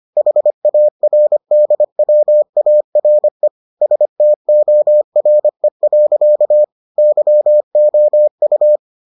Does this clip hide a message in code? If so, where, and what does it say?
Yes: Morse code from 0.2 to 8.8 s, heard as a keyed tone, about 25 wpm, 600 Hz. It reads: HARDWARE STORE. YOU